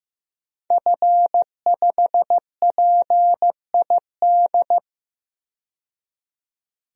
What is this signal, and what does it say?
Morse code, heard as a keyed tone, about 15 wpm, 700 Hz. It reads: F5PID
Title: DXCC-100 Call Signs - 5 Characters - 15wpm